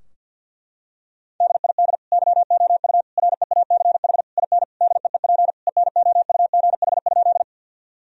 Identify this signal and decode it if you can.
Morse code, heard as a keyed tone, about 49 wpm, 700 Hz. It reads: DID YOU REACH IN DEEP ENOUGH?
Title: Sentences from Top 1000 Words - Review Part 3 - 50wpm